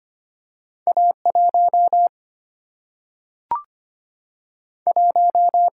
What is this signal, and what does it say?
Morse code, heard as a keyed tone, about 25 wpm, 700 Hz. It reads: A 1 1
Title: Single-Letter Single-Number 25wpm